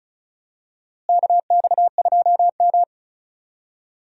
Morse code, heard as a keyed tone, about 36 wpm, 700 Hz. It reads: KX2M
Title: USA Call Signs - 4 Characters - Repeated 3x - 35wpm